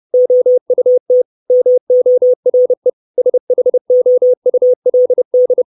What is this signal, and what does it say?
Morse code, heard as a keyed tone, about 30 wpm, 500 Hz. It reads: OUT MORE SHOULD
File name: 632 Words 30WPM